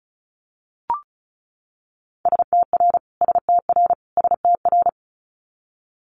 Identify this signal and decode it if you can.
Morse code, heard as a keyed tone, about 34 wpm, 700 Hz. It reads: STR STR STR
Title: Multi-letter Phonemes Repeated 3x 35wpm